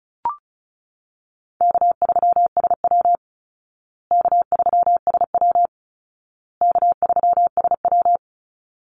Morse code, heard as a keyed tone, about 34 wpm, 700 Hz. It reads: K3SW K3SW K3SW
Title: Easy Call Signs Repeated 3x - 35wpm